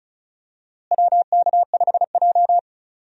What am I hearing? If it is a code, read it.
Morse code, heard as a keyed tone, about 36 wpm, 700 Hz. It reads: WK5J
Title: USA Call Signs - 4 Characters - Repeated 2x - 35wpm